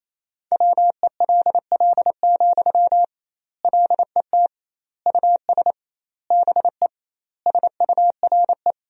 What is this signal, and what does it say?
Morse code, heard as a keyed tone, about 28 wpm, 700 Hz. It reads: WELL, LET US BE SURE
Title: 2x Word-Spacing - Sentences from Top 300 Words - 28wpm